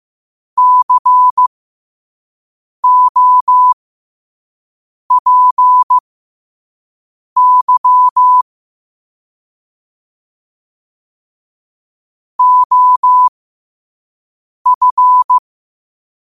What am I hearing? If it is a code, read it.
Morse code, heard as a keyed tone, about 15 wpm, spaced as for 5 wpm, 1000 Hz. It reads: COPY OF
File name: qotdfiles / 5wpm